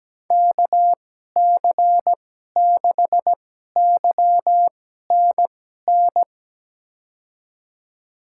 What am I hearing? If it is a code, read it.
Morse code, heard as a keyed tone, about 17 wpm, spaced as for 12 wpm, 700 Hz. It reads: KC6YNN
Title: USA Call Signs - 6 Characters - 2x Character Spacing - 17wpm